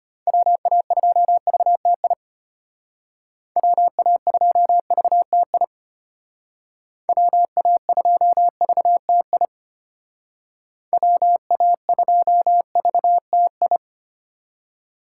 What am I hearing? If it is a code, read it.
Morse code, heard as a keyed tone, about 32 wpm, 700 Hz. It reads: WA2VTI WA2VTI WA2VTI WA2VTI